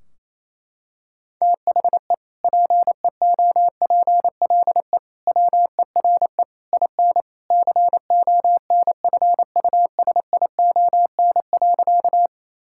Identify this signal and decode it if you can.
Morse code, heard as a keyed tone, about 28 wpm, 700 Hz. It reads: THE PEOPLE WERE IN CONFUSION.
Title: Sentences from Top 1600 Words 28wpm